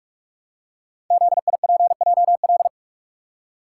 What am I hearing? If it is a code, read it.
Morse code, heard as a keyed tone, about 46 wpm, 700 Hz. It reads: ZIPJL
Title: Code Groups - 5 Letter:Numbers - 45wpm